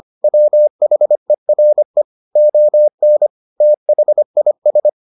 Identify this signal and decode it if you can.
Morse code, heard as a keyed tone, about 25 wpm, 600 Hz. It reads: WHERE ON THIS